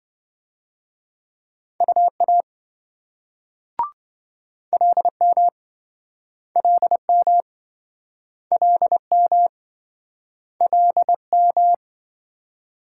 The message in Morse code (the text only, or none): UA LM LM LM LM